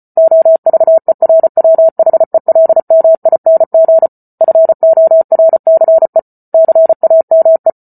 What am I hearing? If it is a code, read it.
Morse code, heard as a keyed tone, about 34 wpm, 650 Hz. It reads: OVERWHELMING FORCE CAME
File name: Book Two / 08-Chapter 8 - 34 WPM